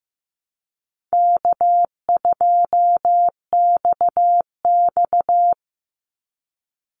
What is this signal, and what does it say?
Morse code, heard as a keyed tone, about 15 wpm, 700 Hz. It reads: K2XX
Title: Easy Call Signs - 15wpm